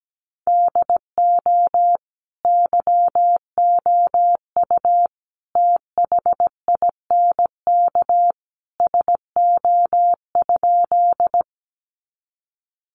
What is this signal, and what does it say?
Morse code, heard as a keyed tone, about 17 wpm, 700 Hz. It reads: DO YOU THINK SO?
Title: Sentences from Top 200 Words 17wpm